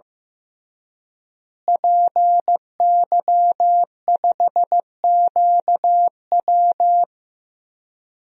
Morse code, heard as a keyed tone, about 15 wpm, 700 Hz. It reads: PY5QW
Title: DXCC-100 Call Signs - 5 Characters - Repeated 3x - 15wpm